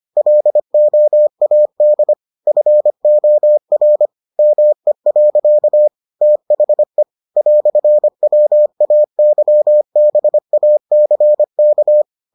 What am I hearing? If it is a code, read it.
Morse code, heard as a keyed tone, about 25 wpm, 600 Hz. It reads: LOAD FOR ME. THE "WAYBACK